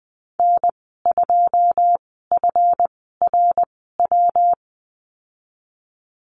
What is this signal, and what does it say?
Morse code, heard as a keyed tone, about 20 wpm, spaced as for 14 wpm, 700 Hz. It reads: N2FRW
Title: 147 Lesson 24g - Letter y - Call Signs 20wpm